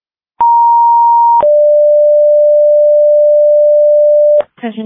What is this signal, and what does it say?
Two-tone page, heard as a one-second tone, then a three-second tone, about 940 then 590 Hz, with speech. It is heard from 0.4 to 4.4 s.